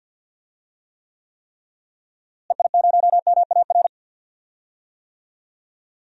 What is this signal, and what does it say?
Morse code, heard as a keyed tone, about 45 wpm, 700 Hz. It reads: EI0MAR
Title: DXCC-100 Call Signs - 6 Characters - Repeated 3x - 50wpm